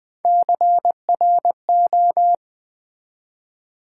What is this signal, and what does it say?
Morse code, heard as a keyed tone, about 20 wpm, 700 Hz. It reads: CRO
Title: Build A Word - 1 Letter At A Time - 4-Letter Words - 20wpm